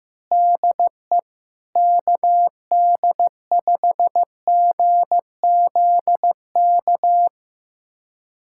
Morse code, heard as a keyed tone, about 15 wpm, 700 Hz. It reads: DE KD5GZK
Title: USA Callsign DE Callsign 15wpm